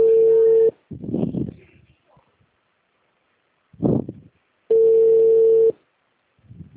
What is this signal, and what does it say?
Ringback tone (China, 450 Hz), with speech.
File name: out- -2011- - - .64383